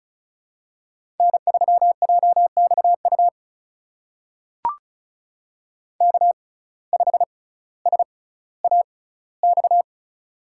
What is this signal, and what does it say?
Morse code, heard as a keyed tone, about 34 wpm, spaced as for 18 wpm, 700 Hz. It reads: N3JXU K5SAX
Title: USA Call Signs - 5 Characters - 6x Character Spacing - 35wpm